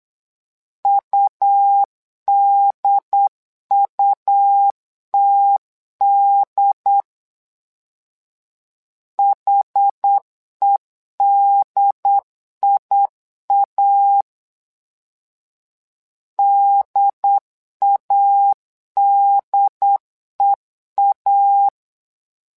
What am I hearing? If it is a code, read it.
Morse code, heard as a keyed tone, about 8 wpm, 800 Hz. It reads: UDUTD HEDIA DADEA
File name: LEZIONI / 2 UHDN / 2 UHD 8wpm 2258